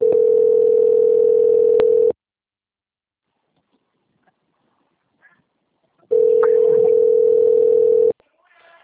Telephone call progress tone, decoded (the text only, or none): ringback tone